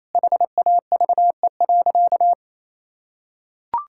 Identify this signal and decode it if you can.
Morse code, heard as a keyed tone, about 28 wpm, 700 Hz. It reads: HAVE.
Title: Sentences from Top 1000 Words - Review Part 6 - 28wpm